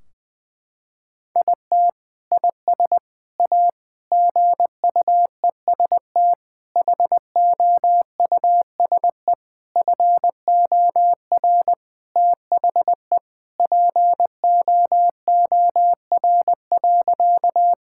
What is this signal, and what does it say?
Morse code, heard as a keyed tone, about 20 wpm, 700 Hz. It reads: IT IS A GUEST HOUSE FOR THE POOR.
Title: Sentences from Top 1600 Words 20wpm